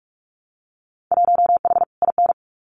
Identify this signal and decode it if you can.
Morse code, heard as a keyed tone, about 46 wpm, 700 Hz. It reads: 1H IN
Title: Winter Field Day Exchanges 45wpm